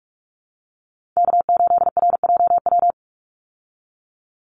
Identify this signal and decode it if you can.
Morse code, heard as a keyed tone, about 44 wpm, 700 Hz. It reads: K8RJW
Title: Call Signs - 45wpm